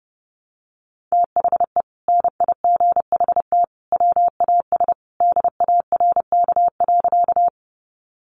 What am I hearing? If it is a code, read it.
Morse code, heard as a keyed tone, about 30 wpm, 700 Hz. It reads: THE NIGHT WAS DARK.
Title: Sentences from Top 500 Words 30wpm